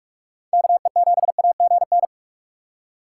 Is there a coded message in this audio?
Yes: Morse code from 0.5 to 2.1 s, heard as a keyed tone, about 57 wpm, 700 Hz. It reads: KE7AGN